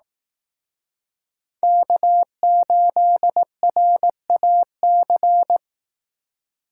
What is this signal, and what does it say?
Morse code, heard as a keyed tone, about 18 wpm, 700 Hz. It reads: K8RAC
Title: Speed Racing - USA 1x3 Call Signs 15wpm